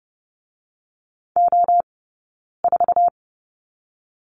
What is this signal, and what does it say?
Morse code, heard as a keyed tone, about 30 wpm, 700 Hz. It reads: O 4
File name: 092 Lesson 16d - Letter c - Two Characters 30wpm